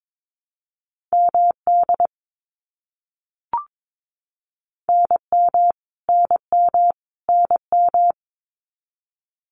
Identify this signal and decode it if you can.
Morse code, heard as a keyed tone, about 22 wpm, 700 Hz. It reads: MD NM NM NM
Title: US States Repeated 3x 22wpm